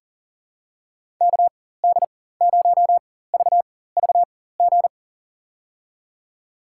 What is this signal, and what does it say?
Morse code, heard as a keyed tone, about 40 wpm, spaced as for 19 wpm, 700 Hz. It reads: KD0VVG
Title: USA Call Signs - 4x Character Spacing - 40wpm